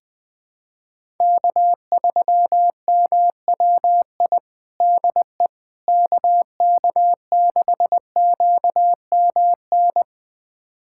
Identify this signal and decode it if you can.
Morse code, heard as a keyed tone, about 21 wpm, 700 Hz. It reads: K3MWI DE KK6QMN
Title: USA Callsign DE Callsign 20wpm